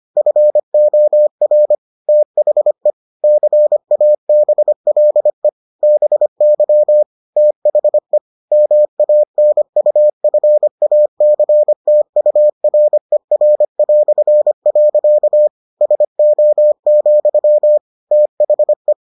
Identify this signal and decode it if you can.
Morse code, heard as a keyed tone, about 25 wpm, 600 Hz. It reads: FOR THE CABLE BY THE MANUFACTURER". SO, THE